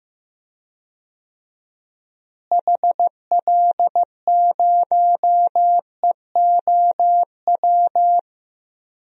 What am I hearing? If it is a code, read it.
Morse code, heard as a keyed tone, about 15 wpm, 700 Hz. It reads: HL0EOW
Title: DXCC-100 Call Signs - 6 Characters - Repeated 2x - 15wpm